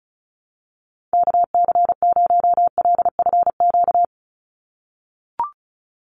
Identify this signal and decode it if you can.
Morse code, heard as a keyed tone, about 34 wpm, 700 Hz. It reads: KC0LFQ
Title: USA Call Signs 35wpm